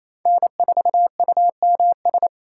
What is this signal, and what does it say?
Morse code, heard as a keyed tone, about 28 wpm, 700 Hz. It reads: N4UMS
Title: USA Call Signs - 5 Characters - 4x Character Spacing - 28wpm